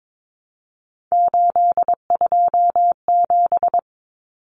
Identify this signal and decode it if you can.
Morse code, heard as a keyed tone, about 22 wpm, 700 Hz. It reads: 827
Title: Common Cut Numbers - 3-Digit 22wpm